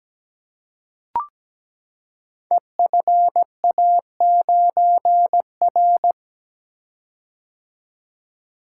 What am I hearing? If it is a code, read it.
Morse code, heard as a keyed tone, about 17 wpm, 700 Hz. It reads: EFA9R
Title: Code Groups - 5 Letter:Numbers - 17wpm